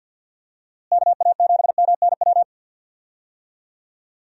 Morse code, heard as a keyed tone, about 51 wpm, 700 Hz. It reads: KA7MNW
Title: USA Call Signs - 6 Characters - Repeated 3x - 50wpm